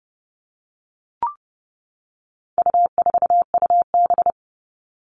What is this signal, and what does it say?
Morse code, heard as a keyed tone, about 30 wpm, 700 Hz. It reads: U4UB
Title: All Cut Numbers - 4-Digit 30wpm